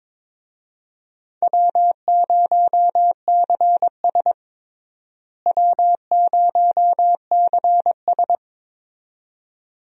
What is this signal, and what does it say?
Morse code, heard as a keyed tone, about 22 wpm, 700 Hz. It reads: W0CS W0CS
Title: USA Call Signs - 4 Characters - Repeated 2x - 22wpm